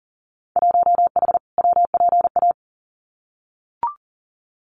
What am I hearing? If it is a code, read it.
Morse code, heard as a keyed tone, about 40 wpm, 700 Hz. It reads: 1H WPA
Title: Winter Field Day Exchanges 40wpm